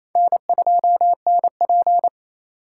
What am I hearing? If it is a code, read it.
Morse code, heard as a keyed tone, about 28 wpm, 700 Hz. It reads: N2NP
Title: USA Call Signs - 4 Characters - 6x Character Spacing - 28wpm